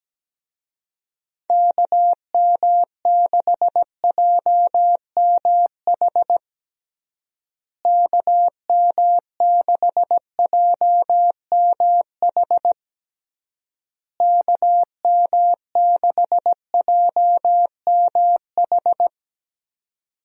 Morse code, heard as a keyed tone, about 17 wpm, 700 Hz. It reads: KM6JMH KM6JMH KM6JMH
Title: USA Call Signs - 6 Characters - Repeated 3x - 17wpm